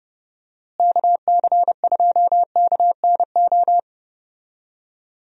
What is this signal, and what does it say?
Morse code, heard as a keyed tone, about 30 wpm, 700 Hz. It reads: KC2KNO
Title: USA Call Signs - 6 Characters - Repeated 2x - 30wpm